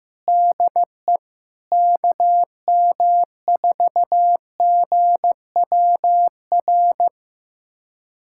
Morse code, heard as a keyed tone, about 16 wpm, 700 Hz. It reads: DE KM4GWR
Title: USA Callsign DE Callsign - 6 Characters - 15wpm